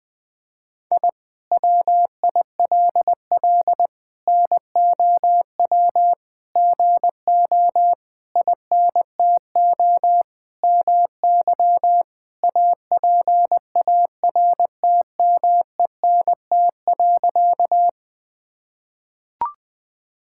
Morse code, heard as a keyed tone, about 20 wpm, 700 Hz. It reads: I WILL NOW GO INTO MY APARTMENT.
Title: Sentences from Top 1000 Words - Review Part 6 - 20wpm